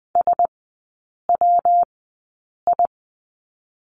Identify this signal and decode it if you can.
Morse code, heard as a keyed tone, about 20 wpm, spaced as for 8 wpm, 700 Hz. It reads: SWI
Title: 191 Lesson 31e - Letter v - Three Characters 20wpm